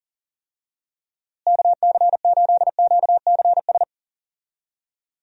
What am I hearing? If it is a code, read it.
Morse code, heard as a keyed tone, about 37 wpm, 700 Hz. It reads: KC8QCS
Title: USA Call Signs - 6 Characters - Repeated 3x - 40wpm